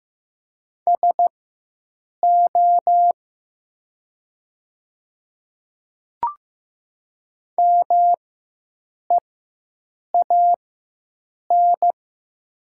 Morse code, heard as a keyed tone, about 15 wpm, spaced as for 7 wpm, 700 Hz. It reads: SO MEAN
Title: 4x Character Spacing - Top 200 Words - 15wpm